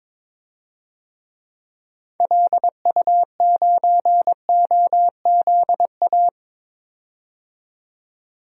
Morse code, heard as a keyed tone, about 22 wpm, 700 Hz. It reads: LU9OZA
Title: DXCC-100 Call Signs - 6 Characters - 22wpm